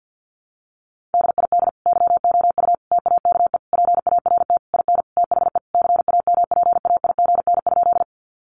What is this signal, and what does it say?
Morse code, heard as a keyed tone, about 50 wpm, 700 Hz. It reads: DID YOU TAKE PART IN THE CAMPAIGN?
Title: Sentences from Top 1200 Words 50wpm